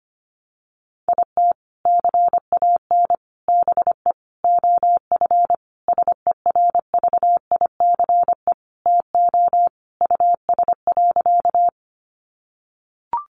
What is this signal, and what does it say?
Morse code, heard as a keyed tone, about 25 wpm, 700 Hz. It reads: IT CAN BE OF SERVICE TO US.
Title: Sentences from Top 1200 Words 25wpm